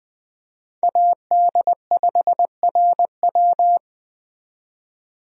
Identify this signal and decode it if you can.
Morse code, heard as a keyed tone, about 20 wpm, 700 Hz. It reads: AD5RW
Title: USA Call Signs - 5 Characters - 20wpm